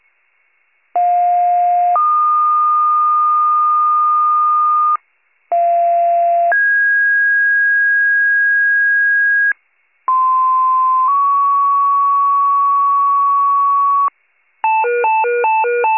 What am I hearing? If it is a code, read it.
Two-tone page, heard as a one-second tone, then a three-second tone, about 710 then 1220 Hz, about 690 then 1650 Hz, about 1040 then 1120 Hz. It is heard from 0.9 to 14.1 s.